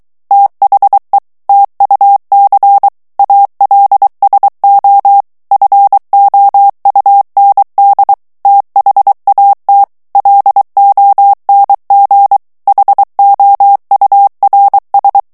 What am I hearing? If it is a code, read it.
Morse code, heard as a keyed tone, about 23 wpm, 800 Hz. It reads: THE TUC ALSO FOUND THAT LONG HOURS
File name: morse mp3 / ANGLAIS texte clair 1 vitesse 22 mots minute